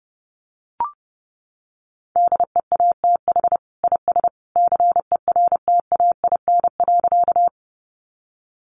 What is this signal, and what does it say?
Morse code, heard as a keyed tone, about 30 wpm, 700 Hz. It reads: DEATH IS CERTAIN.
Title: Sentences from Top 1100 Words 30wpm